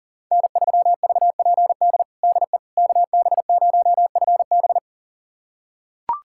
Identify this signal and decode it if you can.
Morse code, heard as a keyed tone, about 40 wpm, 700 Hz. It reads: N3VPD DE KB0FB
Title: USA Callsign DE Callsign - 5 Characters - 40wpm